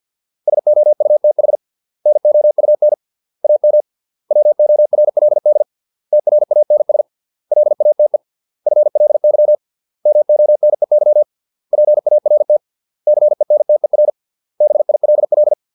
Morse code, heard as a keyed tone, about 50 wpm, 600 Hz. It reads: SOUTH NOUN AM WORLD TRANS LATE FLY MONEY PART CENTER BILL